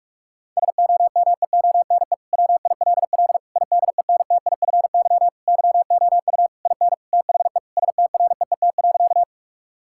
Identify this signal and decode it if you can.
Morse code, heard as a keyed tone, about 46 wpm, 700 Hz. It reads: SOMEONE WILL IDENTIFY YOU IN THE STREET.
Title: Sentences from Top 1000 Words - Review Part 6 - 45wpm